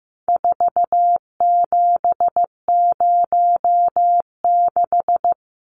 Morse code, heard as a keyed tone, about 14 wpm, 700 Hz. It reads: 4706
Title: Numbers - 4-Digit 15wpm